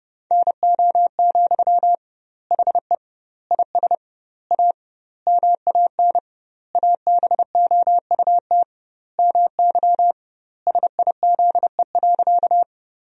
Morse code, heard as a keyed tone, about 30 wpm, 700 Hz. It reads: NO, HE IS A MAN ABOUT MY SIZE.
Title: 2x Word-Spacing - Sentences from Top 400 Words - 30wpm